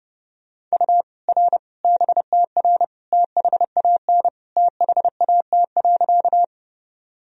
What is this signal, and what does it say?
Morse code, heard as a keyed tone, about 30 wpm, 700 Hz. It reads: U R BTR THAN THAT.
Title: Sentences from Top 400 Words and Abbreviations 30wpm